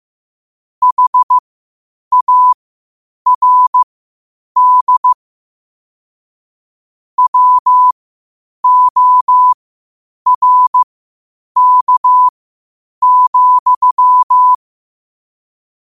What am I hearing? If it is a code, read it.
Morse code, heard as a keyed tone, about 15 wpm, spaced as for 8 wpm, 1000 Hz. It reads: HARD WORK,
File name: qotdfiles / 8wpm